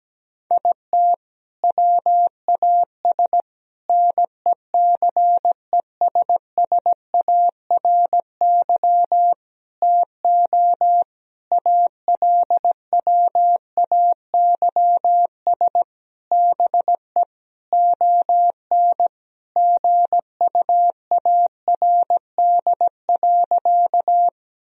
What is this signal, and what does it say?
Morse code, heard as a keyed tone, about 17 wpm, 700 Hz. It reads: IT WAS NECESSARY TO ALWAYS BE ON GUARD.
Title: Sentences from Top 1000 Words - Review Part 4 - 17wpm